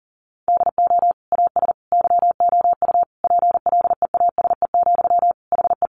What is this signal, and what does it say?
Morse code, heard as a keyed tone, about 40 wpm, 700 Hz. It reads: DO AS YOU PLEASE, HE
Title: Sentences from Top 1500 Words 40wpm